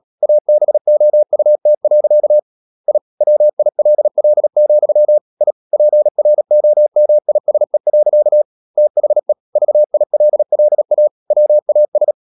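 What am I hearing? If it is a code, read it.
Morse code, heard as a keyed tone, about 37 wpm, 600 Hz. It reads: ABOUT. I WILL, I PROMISE. THE VILLA WAS
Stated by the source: Morse code practice at 35–39 wpm sending English text of Ken Follett's novel Never, from CW med Gnister.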